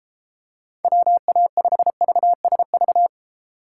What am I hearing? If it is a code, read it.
Morse code, heard as a keyed tone, about 32 wpm, 700 Hz. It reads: WA5VSV